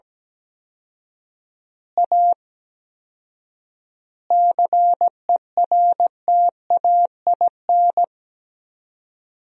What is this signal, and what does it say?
Morse code, heard as a keyed tone, about 17 wpm, 700 Hz. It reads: A CERTAIN
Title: 4x Word-Spacing - Sentences from Top 500 Words - 17wpm